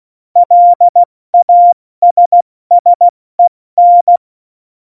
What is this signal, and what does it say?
Morse code, heard as a keyed tone, about 16 wpm, 700 Hz. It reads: LASSEN
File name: Morsekurs / Witze / Witz 18. September 2018, 70 F80